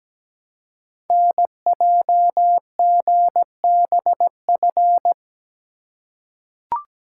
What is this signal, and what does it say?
Morse code, heard as a keyed tone, about 17 wpm, 700 Hz. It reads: NJGBF
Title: Code Groups - 5 Letters - 17wpm